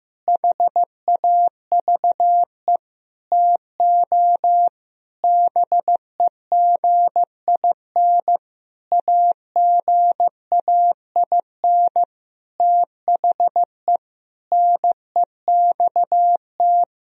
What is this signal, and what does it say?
Morse code, heard as a keyed tone, about 15 wpm, 700 Hz. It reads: HAVE TO BEGIN AGAIN THE NEXT
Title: Sentences from Top 1000 Words - Review Part 1 - 15wpm